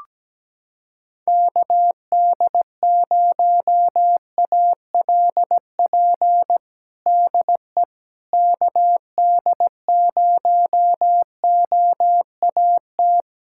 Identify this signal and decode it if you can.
Morse code, heard as a keyed tone, about 17 wpm, 700 Hz. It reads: KD0ALP DE KD0OAT
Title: USA Callsign DE Callsign 17wpm